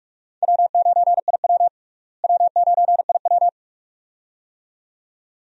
Morse code, heard as a keyed tone, about 46 wpm, 700 Hz. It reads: W9IW W9IW
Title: USA Call Signs - 4 Characters - Repeated 2x - 45wpm